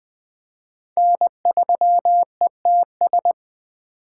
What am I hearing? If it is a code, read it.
Morse code, heard as a keyed tone, about 20 wpm, 700 Hz. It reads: N3ETS
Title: USA Call Signs - 5 Characters - 3x Character Spacing - 20wpm